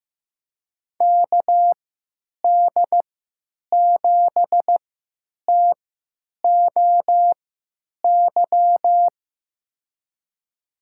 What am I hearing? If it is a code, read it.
Morse code, heard as a keyed tone, about 15 wpm, spaced as for 9 wpm, 700 Hz. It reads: KD7TOY
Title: USA Call Signs - 6 Characters - 3x Character Spacing - 15wpm